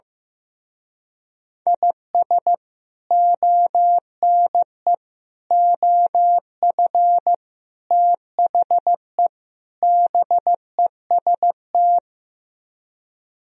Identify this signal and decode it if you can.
Morse code, heard as a keyed tone, about 15 wpm, 700 Hz. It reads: IS ONE OF THE BEST
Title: Speed-Racing - Sets of 5 Words 15wpm